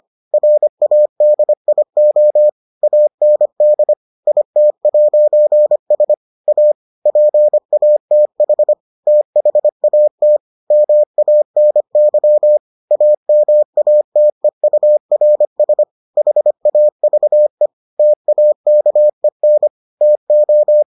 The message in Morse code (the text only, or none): RADIO AND IT'S A PATH THAT MANY AMATEURS HAVE TAKEN TO